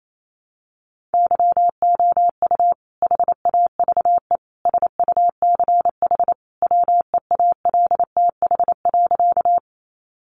Morse code, heard as a keyed tone, about 28 wpm, 700 Hz. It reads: YOU HAVE SUCH WEALTH.
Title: Sentences from Top 1200 Words 28wpm